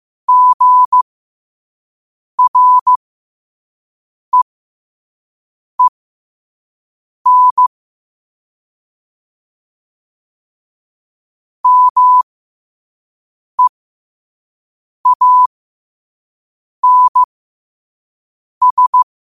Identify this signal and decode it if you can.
Morse code, heard as a keyed tone, about 15 wpm, spaced as for 5 wpm, 1000 Hz. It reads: GREEN MEANS
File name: qotdfiles / 5wpm